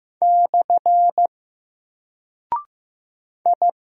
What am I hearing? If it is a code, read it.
Morse code, heard as a keyed tone, about 15 wpm, 700 Hz. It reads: / I
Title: Single Character (Letter:Number:Punctuation:Prosign) Alt 15wpm